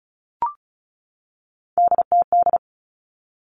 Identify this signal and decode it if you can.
Morse code, heard as a keyed tone, about 36 wpm, 700 Hz. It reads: DTD
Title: Three Challenging Letter-Number Combinations 35wpm